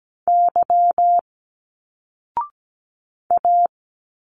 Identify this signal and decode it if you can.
Morse code, heard as a keyed tone, about 17 wpm, 700 Hz. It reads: Y A
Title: Single Letters 17wpm